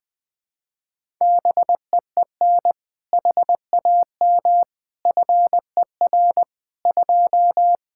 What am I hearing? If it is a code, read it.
Morse code, heard as a keyed tone, about 20 wpm, 700 Hz. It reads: BEEN HAM FER 2
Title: Practice QSO - Exchange 3 - More Info - 20wpm